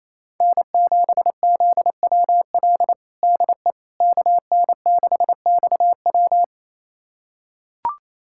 Morse code, heard as a keyed tone, about 28 wpm, 700 Hz. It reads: N7ZWL DE KN6XW
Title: USA Callsign DE Callsign - 5 Characters - 28wpm